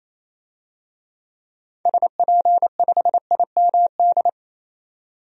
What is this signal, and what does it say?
Morse code, heard as a keyed tone, about 28 wpm, 700 Hz. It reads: SP5IMD
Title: DXCC-100 Call Signs - 6 Characters - 28wpm